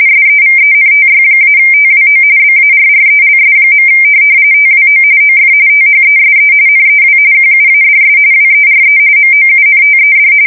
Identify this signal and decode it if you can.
RTTY, about 45.45 baud. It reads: SEVENTY-TWO BARRELS. TWO OF THESE NEW QUADS WERE MOUNTED
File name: fsk170 / USS-Massachusetts